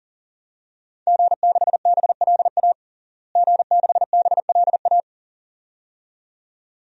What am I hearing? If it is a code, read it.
Morse code, heard as a keyed tone, about 40 wpm, 700 Hz. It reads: G6BLA G6BLA
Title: DXCC-100 Call Signs - 5 Characters - Repeated 2x - 40wpm